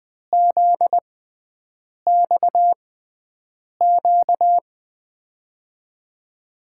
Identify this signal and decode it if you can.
Morse code, heard as a keyed tone, about 20 wpm, spaced as for 7 wpm, 700 Hz. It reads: ZXQ
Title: USA Call Signs - 5 Characters - 6x Character Spacing - 20wpm